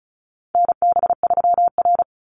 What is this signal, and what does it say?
Morse code, heard as a keyed tone, about 34 wpm, 700 Hz. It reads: NB3R
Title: Call Signs Repeated 3x - 35wpm